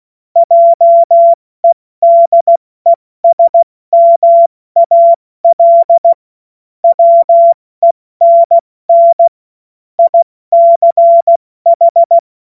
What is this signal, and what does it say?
Morse code, heard as a keyed tone, about 16 wpm, 680 Hz. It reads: JEDESMAL WENN ICH
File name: Morsekurs / Witze / Witz 09 April 2024 70 F80